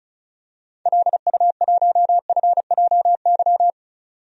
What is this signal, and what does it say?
Morse code, heard as a keyed tone, about 36 wpm, 700 Hz. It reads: LU1FJY
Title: DXCC-100 Call Signs - 6 Characters - 35wpm